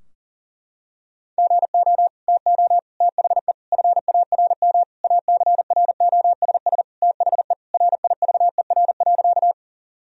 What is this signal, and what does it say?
Morse code, heard as a keyed tone, about 40 wpm, 700 Hz. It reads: GO TO THE FARM ACROSS THE RIVER.
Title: Sentences from Top 1000 Words - Review Part 2 - 40wpm